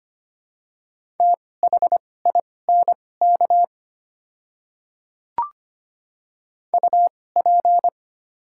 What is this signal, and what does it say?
Morse code, heard as a keyed tone, about 25 wpm, spaced as for 18 wpm, 700 Hz. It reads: THINK UP
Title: 2x Character Spacing - Top 200 Words - 25wpm